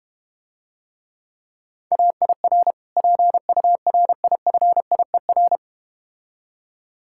Morse code, heard as a keyed tone, about 32 wpm, 700 Hz. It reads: AIR PURIFIER